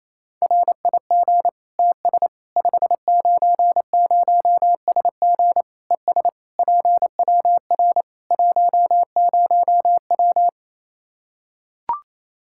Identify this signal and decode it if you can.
Morse code, heard as a keyed tone, about 27 wpm, 700 Hz. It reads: RIG TS 590SG ES PWR 10W
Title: Practice QSO - Exchange 2 - Info - 28wpm